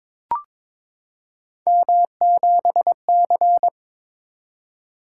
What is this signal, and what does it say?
Morse code, heard as a keyed tone, about 22 wpm, 700 Hz. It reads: M7C
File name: practice / 22wpm / Three Letter-Number Combinations - 22wpm